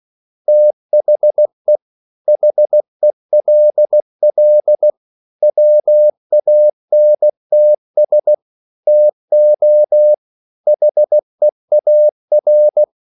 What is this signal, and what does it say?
Morse code, heard as a keyed tone, about 16 wpm, 600 Hz. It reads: THE HELL WANTS TO HEAR
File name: CW / Practice / EnglishQuotations0416